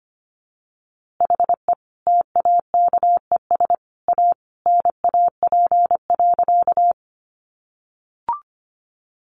Morse code, heard as a keyed tone, about 25 wpm, 700 Hz. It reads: HE TAKES A NAP.
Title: Easy Phrases 25wpm